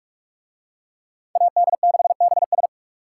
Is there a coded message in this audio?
Yes: Morse code from 1.4 to 2.7 s, heard as a keyed tone, about 43 wpm, 700 Hz. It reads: AD6BS